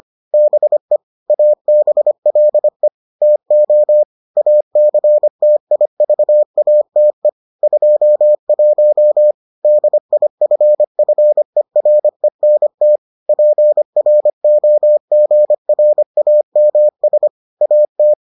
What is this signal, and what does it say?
Morse code, heard as a keyed tone, about 25 wpm, 600 Hz. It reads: BE ABLE TO ACTIVATE 21 DIFFERENT PROGRAMS AT